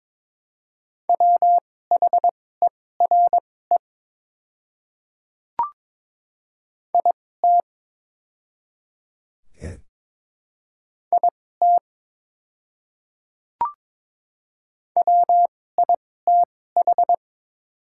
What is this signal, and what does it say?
Morse code, heard as a keyed tone, about 22 wpm, spaced as for 16 wpm, 700 Hz. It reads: WHERE IT IT WITH
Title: 2x Character Spacing - Top 200 Words - 22wpm